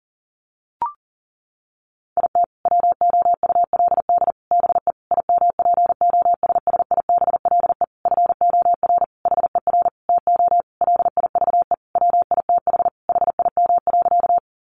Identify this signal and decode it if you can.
Morse code, heard as a keyed tone, about 40 wpm, 700 Hz. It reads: IT WOULD BE IMPOSSIBLE FOR HER TO LIVE WITH HIM.
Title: Sentences from Top 1100 Words 40wpm